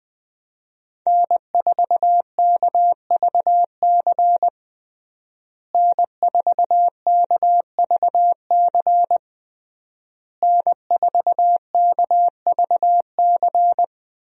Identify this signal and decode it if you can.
Morse code, heard as a keyed tone, about 20 wpm, 700 Hz. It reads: N4KVC N4KVC N4KVC
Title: USA Call Signs - 5 Characters - Repeated 3x - 20wpm